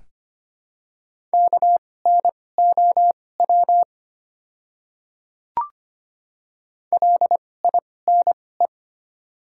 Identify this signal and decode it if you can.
Morse code, heard as a keyed tone, about 25 wpm, spaced as for 18 wpm, 700 Hz. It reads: KNOW LINE
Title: 2x Character Spacing - Top 200 Words - 25wpm